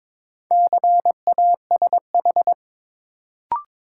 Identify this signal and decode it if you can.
Morse code, heard as a keyed tone, about 22 wpm, 700 Hz. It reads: CASH
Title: Sentences from Top 700 Words - New Word Drill - 22wpm